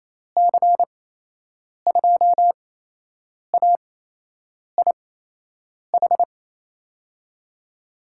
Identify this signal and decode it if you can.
Morse code, heard as a keyed tone, about 28 wpm, spaced as for 8 wpm, 700 Hz. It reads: C2AIH
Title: USA Call Signs - 6 Characters - 8x Character Spacing - 28wpm